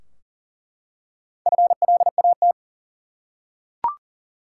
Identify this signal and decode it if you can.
Morse code, heard as a keyed tone, about 40 wpm, 700 Hz. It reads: FLAT
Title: Four Letter Words - 40wpm